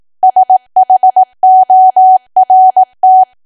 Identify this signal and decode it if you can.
Morse code, heard as a keyed tone, about 18 wpm, 750 Hz. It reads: SHORT
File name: Archive / 18 WPM / 18WPM